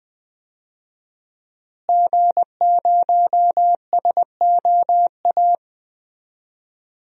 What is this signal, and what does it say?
Morse code, heard as a keyed tone, about 20 wpm, 700 Hz. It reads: G0SOA
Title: DXCC-100 Call Signs - 5 Characters - Repeated 3x - 20wpm